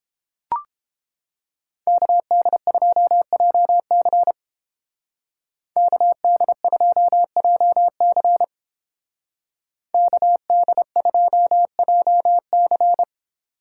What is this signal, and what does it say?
Morse code, heard as a keyed tone, about 30 wpm, 700 Hz. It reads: KD2JC KD2JC KD2JC